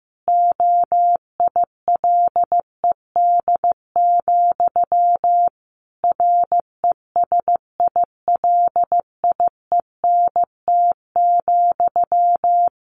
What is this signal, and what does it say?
Morse code, heard as a keyed tone, about 14 wpm, 700 Hz. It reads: OILED, RESILIENT,
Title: A Sound Of Thunder 15wpm